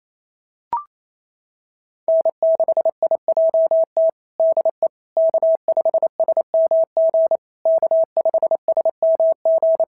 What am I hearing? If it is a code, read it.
Morse code, heard as a keyed tone, about 28 wpm, 650 Hz. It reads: N6IJT DE K5SMG K5SMG
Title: SOTA QSO Practice 28wpm